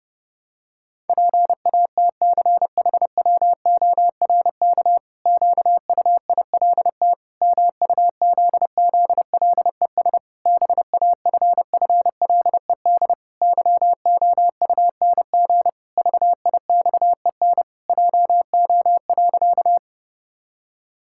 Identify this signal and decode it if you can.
Morse code, heard as a keyed tone, about 30 wpm, 700 Hz. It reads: PATCHWORK QUILT MUZZLES BAFFLED YOUNG VIXEN JO.
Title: Pangram Challenge v2 - 30wpm